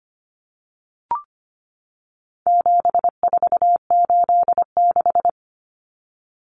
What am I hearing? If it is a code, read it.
Morse code, heard as a keyed tone, about 25 wpm, 700 Hz. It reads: 7486
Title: Common Cut Numbers - 4-Digit 25wpm